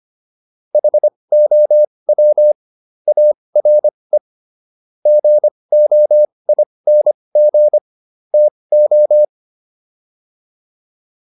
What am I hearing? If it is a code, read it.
Morse code, heard as a keyed tone, about 25 wpm, spaced as for 20 wpm, 600 Hz. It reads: HOW ARE GOING TO